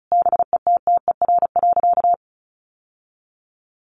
Morse code, heard as a keyed tone, about 34 wpm, 700 Hz. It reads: BETTER.
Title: Sentences from Top 1100 Words 35wpm